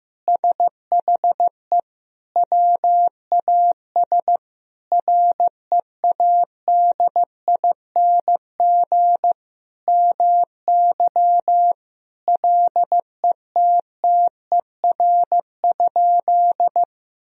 Morse code, heard as a keyed tone, about 15 wpm, 700 Hz. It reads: SHE WAS READING MY LETTER?
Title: Sentences from Top 1000 Words - Review Part 3 - 15wpm